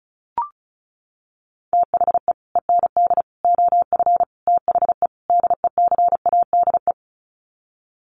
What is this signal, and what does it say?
Morse code, heard as a keyed tone, about 34 wpm, 700 Hz. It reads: THE END OF THE DECADE
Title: Sets of 5 Words Encore 35wpm